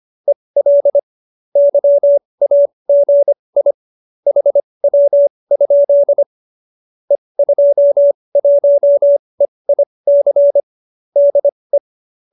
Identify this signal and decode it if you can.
Morse code, heard as a keyed tone, about 25 wpm, spaced as for 20 wpm, 570 Hz. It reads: EL YAGI HW? E21EIC DE